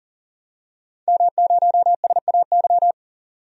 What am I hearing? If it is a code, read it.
Morse code, heard as a keyed tone, about 40 wpm, 700 Hz. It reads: M0SAY
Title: DXCC-100 Call Signs - 5 Characters - 40wpm